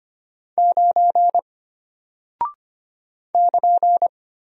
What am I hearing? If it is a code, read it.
Morse code, heard as a keyed tone, about 25 wpm, 700 Hz. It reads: 9 <KN>
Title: Single Character (Letter:Number:Punctuation:Prosign) Alt 25wpm